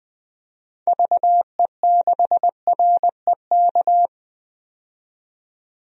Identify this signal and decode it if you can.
Morse code, heard as a keyed tone, about 20 wpm, 700 Hz. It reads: VE6REK
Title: DXCC-100 Call Signs - 6 Characters - 20wpm